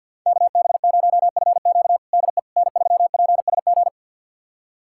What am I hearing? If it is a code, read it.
Morse code, heard as a keyed tone, about 47 wpm, 700 Hz. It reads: KB0FQ DE N3PSG
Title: USA Callsign DE Callsign - 5 Characters - 50wpm